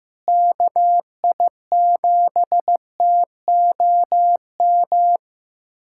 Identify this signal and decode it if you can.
Morse code, heard as a keyed tone, about 15 wpm, 700 Hz. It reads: KI7TOM
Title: USA Callsign DE Callsign - 6 Characters - 15wpm